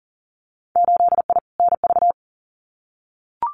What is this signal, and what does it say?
Morse code, heard as a keyed tone, about 40 wpm, 700 Hz. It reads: 8I NV
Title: Winter Field Day Exchanges Encore 40wpm